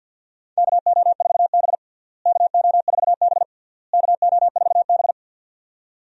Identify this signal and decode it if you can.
Morse code, heard as a keyed tone, about 46 wpm, 700 Hz. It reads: KO4B KO4B KO4B
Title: USA Call Signs - 4 Characters - Repeated 3x - 50wpm